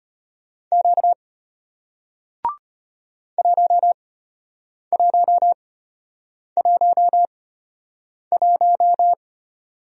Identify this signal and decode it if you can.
Morse code, heard as a keyed tone, about 33 wpm, spaced as for 8 wpm, 690 Hz. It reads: Q 1111